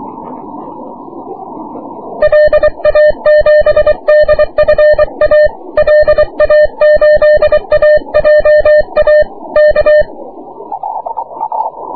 Morse code, heard as a keyed tone, about 23 wpm, 600 Hz. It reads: LA7DFA LA8AJA K
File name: wp / 6m aurora cw 22jul 2009